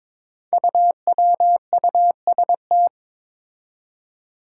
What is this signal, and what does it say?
Morse code, heard as a keyed tone, about 21 wpm, 700 Hz. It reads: UWUST
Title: Five Challenging Letter-Number Combinations 22wpm